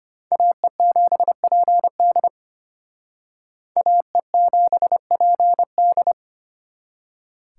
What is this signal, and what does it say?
Morse code, heard as a keyed tone, about 31 wpm, 700 Hz. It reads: AE7PD AE7PD
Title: Speed Racing - USA 2x2 Call Signs 25wpm